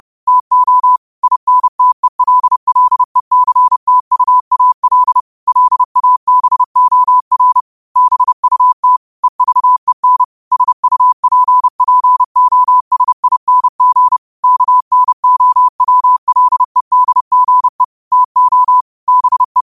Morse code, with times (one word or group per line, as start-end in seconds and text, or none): 0.3-1.0 TO
1.2-5.2 INTELLECTUAL
5.5-7.6 LABOR
8.0-9.0 BUT
9.2-10.2 EVEN
10.5-14.2 SUPPOSING
14.4-17.8 KNOWLEDGE
18.1-18.8 TO
19.1-19.6 BE